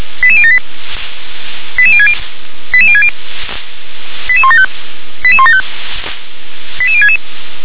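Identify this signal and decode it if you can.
Five-tone selective call (ZVEI), heard as quick runs of five tones, tones of about 70 ms. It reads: 88078 88060 88060 88175 88175 88066
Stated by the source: Five-tone ZVEI selective call received on SDR.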